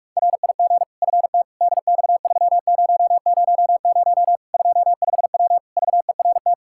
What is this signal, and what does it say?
Morse code, heard as a keyed tone, about 46 wpm, 700 Hz. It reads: RIG FT DX3000 25W VERT
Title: Practice QSO - Exchange 2 - Info - 45wpm